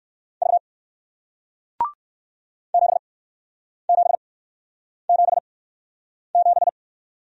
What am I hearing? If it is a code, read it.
Morse code, heard as a keyed tone, about 60 wpm, spaced as for 9 wpm, 700 Hz. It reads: V 7777